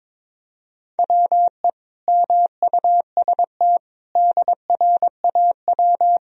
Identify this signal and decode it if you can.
Morse code, heard as a keyed tone, about 22 wpm, 700 Hz. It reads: WE MUST DRAW
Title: Sentences from Top 1000 Words - Review Part 1 - 22wpm